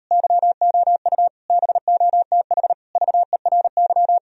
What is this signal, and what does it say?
Morse code, heard as a keyed tone, about 38 wpm, 700 Hz. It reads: YOU BOTH VERY